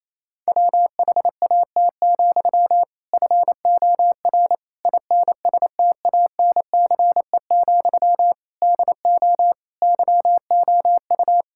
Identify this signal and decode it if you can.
Morse code, heard as a keyed tone, about 28 wpm, 700 Hz. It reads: WHAT, FOR INSTANCE, DO YOU
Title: Sentences from Top 1000 Words - Review Part 5 - 28wpm